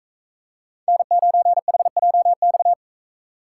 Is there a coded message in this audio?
Yes: Morse code from 0.9 to 2.7 s, heard as a keyed tone, about 42 wpm, 700 Hz. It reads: N9HJX